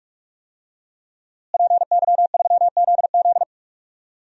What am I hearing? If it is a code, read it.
Morse code, heard as a keyed tone, about 46 wpm, 700 Hz. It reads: PY3ZZ
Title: DXCC-100 Call Signs - 5 Characters - Repeated 2x - 45wpm